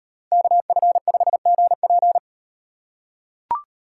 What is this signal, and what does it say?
Morse code, heard as a keyed tone, about 38 wpm, 700 Hz. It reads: KF5GP